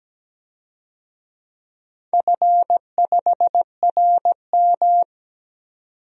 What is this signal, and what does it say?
Morse code, heard as a keyed tone, about 17 wpm, 700 Hz. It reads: F5RM
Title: DXCC-100 Call Signs - 4 Characters - 17wpm